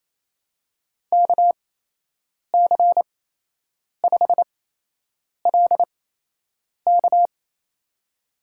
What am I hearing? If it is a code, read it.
Morse code, heard as a keyed tone, about 28 wpm, spaced as for 8 wpm, 700 Hz. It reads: KC5LK
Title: USA Call Signs - 5 Characters - 8x Character Spacing - 28wpm